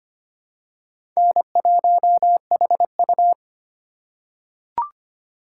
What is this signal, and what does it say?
Morse code, heard as a keyed tone, about 25 wpm, 700 Hz. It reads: N1HU
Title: USA Call Signs - 4 Characters - 2x Character Spacing - 25wpm